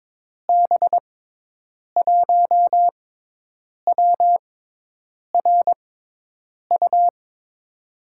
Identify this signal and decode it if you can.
Morse code, heard as a keyed tone, about 22 wpm, spaced as for 8 wpm, 700 Hz. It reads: B1WRU
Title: USA Call Signs - 6 Characters - 6x Character Spacing - 22wpm